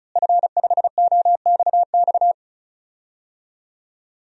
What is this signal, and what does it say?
Morse code, heard as a keyed tone, about 36 wpm, 700 Hz. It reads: F5OXX
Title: DXCC-100 Call Signs - 5 Characters - Repeated 2x - 35wpm